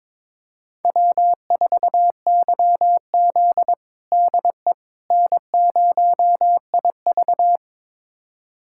Morse code, heard as a keyed tone, about 22 wpm, 700 Hz. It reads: W4YZ DE N0IV
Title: USA Callsign DE Callsign - 4 Characters - 22wpm